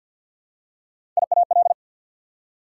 Morse code, heard as a keyed tone, about 48 wpm, 700 Hz. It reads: IAL
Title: Prefixes and Suffixes 50wpm